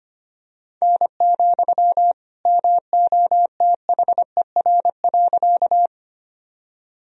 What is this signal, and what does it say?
Morse code, heard as a keyed tone, about 25 wpm, 700 Hz. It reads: N, MOTHER.
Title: Sentences from Top 200 Words and Abbreviations 25wpm